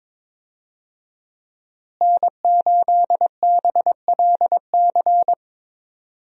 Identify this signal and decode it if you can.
Morse code, heard as a keyed tone, about 22 wpm, 700 Hz. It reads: N8BLC
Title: USA Call Signs - 5 Characters - Repeated 2x - 22wpm